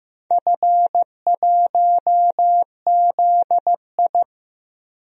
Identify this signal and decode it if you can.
Morse code, heard as a keyed tone, about 15 wpm, 700 Hz. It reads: F1ZI
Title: DXCC-100 Call Signs - 4 Characters - Repeated 3x - 15wpm